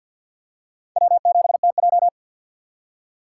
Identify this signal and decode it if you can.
Morse code, heard as a keyed tone, about 52 wpm, 700 Hz. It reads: W7TJ
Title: USA Call Signs - 2x Character Spacing - 50wpm